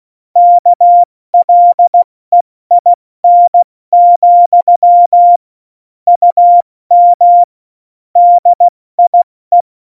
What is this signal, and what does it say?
Morse code, heard as a keyed tone, about 16 wpm, 700 Hz. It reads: KLEIN, UM DIE
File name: Morsekurs / Witze / Witz 12. September 17, Tempo 70